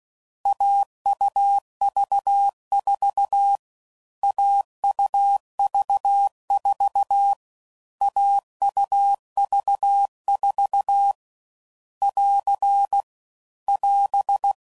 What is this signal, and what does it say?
Morse code, heard as a keyed tone, about 16 wpm, 800 Hz. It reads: AUV4 AUV4 AUV4 <AR> <AS>
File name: Combinations / Difficult Combos / AUV AUV4